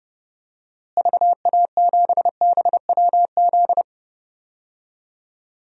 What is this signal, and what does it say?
Morse code, heard as a keyed tone, about 30 wpm, 700 Hz. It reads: VA7BWZ
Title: DXCC-100 Call Signs - 6 Characters - Repeated 2x - 30wpm